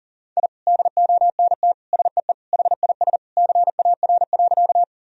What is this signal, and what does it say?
Morse code, heard as a keyed tone, about 40 wpm, 700 Hz. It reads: I DONT SEE HIS CAR.
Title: Sentences from Top 1000 Words - Review Part 1 - 40wpm